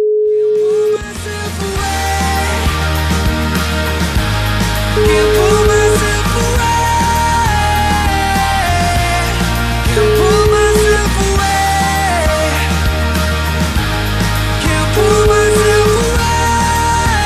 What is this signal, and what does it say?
Ringback tone (Europe, 425 Hz), with noise.